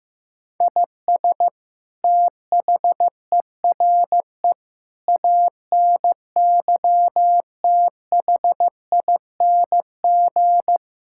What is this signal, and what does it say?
Morse code, heard as a keyed tone, about 15 wpm, 700 Hz. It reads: IS THERE ANYTHING
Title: Sentences from Top 1000 Words - Review Part 6 - 15wpm